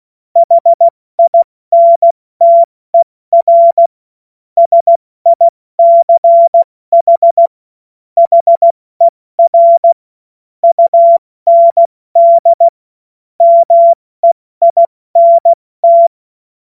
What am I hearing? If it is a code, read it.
Morse code, heard as a keyed tone, about 16 wpm, 680 Hz. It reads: HINTER SICH HER UND MEINT
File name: Morsekurs / Witze / Witz 31. August 2021, 70 F80